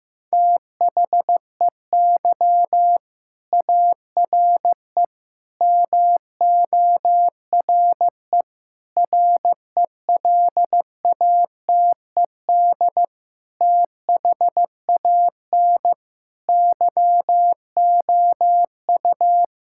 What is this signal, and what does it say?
Morse code, heard as a keyed tone, about 15 wpm, 700 Hz. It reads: THEY ARE MORE RELATED THAN YOU
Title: Sentences from Top 1000 Words - Review Part 5 - 15wpm